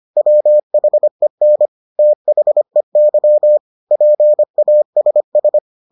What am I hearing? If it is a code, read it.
Morse code, heard as a keyed tone, about 25 wpm, 600 Hz. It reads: WHEN THEY PASS